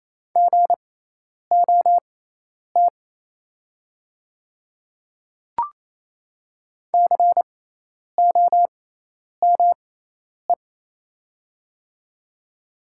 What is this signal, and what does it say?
Morse code, heard as a keyed tone, about 28 wpm, spaced as for 10 wpm, 690 Hz. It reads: GOT COME
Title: 6x Character Spacing - Top 300 Words - 28wpm